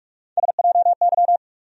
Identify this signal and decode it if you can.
Morse code, heard as a keyed tone, about 46 wpm, 700 Hz. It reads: SJY
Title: USA Call Sign Postfixes - 3 Letters 45wpm